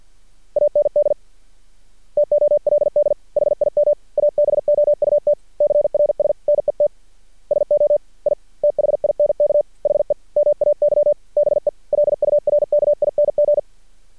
Morse code, heard as a keyed tone, about 48 wpm, 590 Hz. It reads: AND TOLD HIM ABOUT CRS NET SO I THINK HE MAY BE LURKING
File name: archived / CRSnet